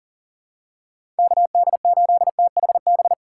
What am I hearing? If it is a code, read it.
Morse code, heard as a keyed tone, about 40 wpm, 700 Hz. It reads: KD8THB
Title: USA Call Signs - 6x Character Spacing - 40wpm